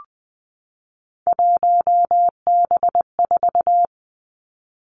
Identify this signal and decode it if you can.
Morse code, heard as a keyed tone, about 20 wpm, 700 Hz. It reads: 1B4
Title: Three Challenging Letter-Number Combinations 20wpm